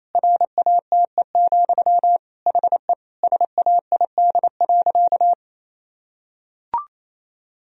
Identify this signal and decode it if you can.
Morse code, heard as a keyed tone, about 27 wpm, 700 Hz. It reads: RATE, HE SAID.
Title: Sentences from Top 1000 Words - Review Part 4 - 28wpm